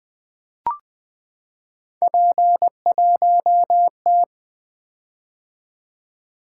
Morse code, heard as a keyed tone, about 20 wpm, 700 Hz. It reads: P1T
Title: Three Letter-Number Combinations - 20wpm